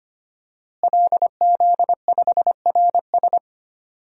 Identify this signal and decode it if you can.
Morse code, heard as a keyed tone, about 25 wpm, 700 Hz. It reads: LZ5RS
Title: DXCC-100 Call Signs - 5 Characters - Repeated 2x - 25wpm